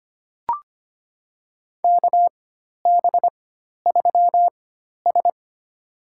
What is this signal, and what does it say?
Morse code, heard as a keyed tone, about 25 wpm, spaced as for 12 wpm, 700 Hz. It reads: KB3S
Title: USA Call Signs - 4 Characters - 4x Character Spacing - 25wpm